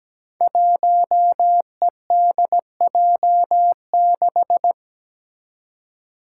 Morse code, heard as a keyed tone, about 17 wpm, 700 Hz. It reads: 1EDJ6
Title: Five Challenging Letter-Number Combinations 17wpm